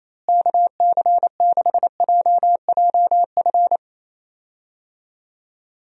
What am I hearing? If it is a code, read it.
Morse code, heard as a keyed tone, about 28 wpm, 700 Hz. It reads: KC6JJF
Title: USA Call Signs - 6 Characters - Repeated 2x - 28wpm